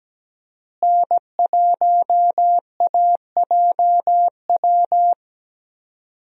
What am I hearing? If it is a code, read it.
Morse code, heard as a keyed tone, about 17 wpm, 700 Hz. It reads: N1AJW
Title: USA Call Signs - 5 Characters - Repeated 2x - 17wpm